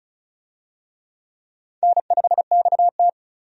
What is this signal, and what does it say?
Morse code, heard as a keyed tone, about 36 wpm, 700 Hz. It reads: N5XT
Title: USA Call Signs - 4 Characters - 35wpm